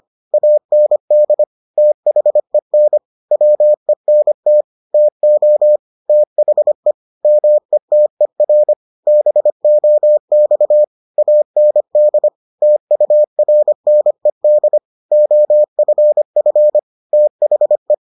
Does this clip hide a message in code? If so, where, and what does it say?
Yes: Morse code from 0.3 to 17.9 s, heard as a keyed tone, about 25 wpm, 600 Hz. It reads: AND THEN WENT TO THE METER BOX AND TURNED OFF THE